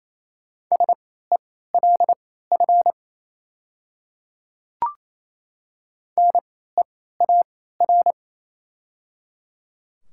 Morse code, heard as a keyed tone, about 27 wpm, spaced as for 16 wpm, 700 Hz. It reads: SELF NEAR
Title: 3x Character Spacing - Top 500 Words - 28wpm